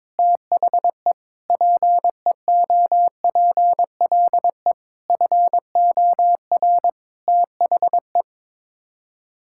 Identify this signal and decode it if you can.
Morse code, heard as a keyed tone, about 22 wpm, 700 Hz. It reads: THE PEOPLE FOR THE
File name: practice / 22wpm / Sets of 8 Words - 22wpm